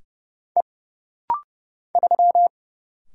Morse code, heard as a keyed tone, about 30 wpm, 700 Hz. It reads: E 3
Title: Single Letter-Number Alt - Rapid-Fire 30wpm